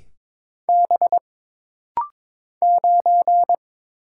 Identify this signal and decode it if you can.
Morse code, heard as a keyed tone, about 22 wpm, 700 Hz. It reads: B 9
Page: This rapid-fire timing is geared to push your ICR (Instant Character Recognition) skills.